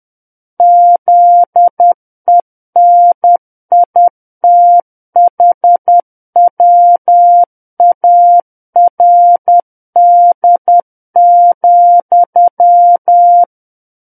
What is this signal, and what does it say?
Morse code, heard as a keyed tone, about 10 wpm, 700 Hz. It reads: ZENITHWARD,
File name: Book One / 01-Chapter 1 - 10 WPM